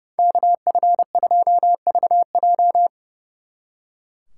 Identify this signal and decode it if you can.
Morse code, heard as a keyed tone, about 37 wpm, 700 Hz. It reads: KF2VJ
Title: Speed Racing - USA 2x2 Call Signs 30wpm